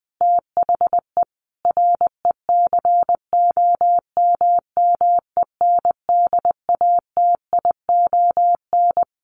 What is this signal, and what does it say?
Morse code, heard as a keyed tone, about 20 wpm, 700 Hz. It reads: THE RECOMMENDATION
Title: Sentences from Top 1400 Words 20wpm